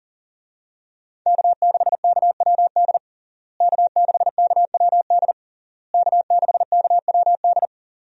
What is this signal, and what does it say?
Morse code, heard as a keyed tone, about 40 wpm, 700 Hz. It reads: K6KWD K6KWD K6KWD
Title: USA Call Signs - 5 Characters - Repeated 3x - 40wpm